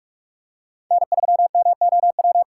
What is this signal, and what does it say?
Morse code, heard as a keyed tone, about 43 wpm, 700 Hz. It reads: N3MOW